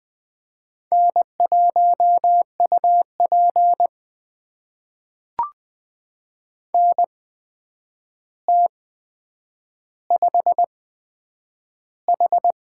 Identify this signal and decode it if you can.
Morse code, heard as a keyed tone, about 20 wpm, spaced as for 9 wpm, 700 Hz. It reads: N1UP NT5H
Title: USA Call Signs - 8x Character Spacing - 20wpm